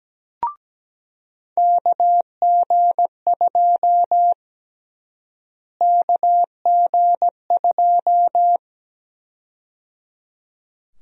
Morse code, heard as a keyed tone, about 17 wpm, 700 Hz. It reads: KG2 KG2
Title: USA Call Sign Prefixes - Repeated 2x - 17wpm